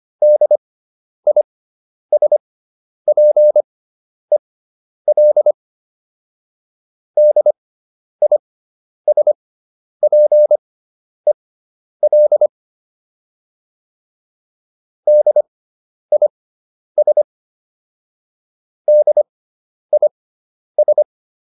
Morse code, heard as a keyed tone, about 25 wpm, spaced as for 10 wpm, 600 Hz. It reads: DISPEL DISPEL DIS DIS